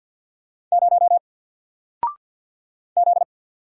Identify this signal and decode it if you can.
Morse code, heard as a keyed tone, about 48 wpm, 700 Hz. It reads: 0 Z
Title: Single Letter-Number 50wpm